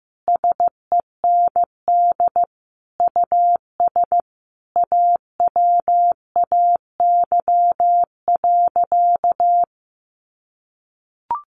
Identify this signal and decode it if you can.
Morse code, heard as a keyed tone, about 15 wpm, 700 Hz. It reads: SEND US AWAY.
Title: Sentences from Top 1500 Words 15wpm